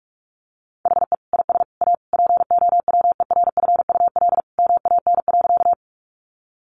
Morse code, heard as a keyed tone, about 44 wpm, 700 Hz. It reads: HE IS A POWERFUL MAN.
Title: Sentences from Top 1300 Words 45wpm